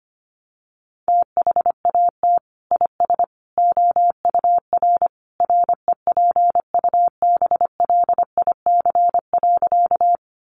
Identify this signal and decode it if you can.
Morse code, heard as a keyed tone, about 25 wpm, 700 Hz. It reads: THAT IS OUR REPUBLIC.
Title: Sentences from Top 1500 Words 25wpm